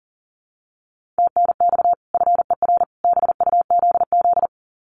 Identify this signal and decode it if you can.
Morse code, heard as a keyed tone, about 40 wpm, 700 Hz. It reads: TNX FER BUZZ
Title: Common QSO Phrases 40wpm